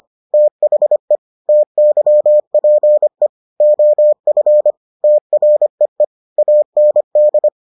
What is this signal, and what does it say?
Morse code, heard as a keyed tone, about 25 wpm, 600 Hz. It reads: THE TYPE OF TREE AND